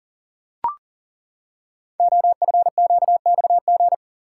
Morse code, heard as a keyed tone, about 41 wpm, 700 Hz. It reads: OFQXG
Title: Code Groups - 5 Letters - 40wpm